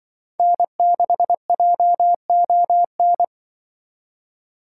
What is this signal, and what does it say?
Morse code, heard as a keyed tone, about 24 wpm, 700 Hz. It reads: N6JON
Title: Speed Racing - USA Call Signs 20wpm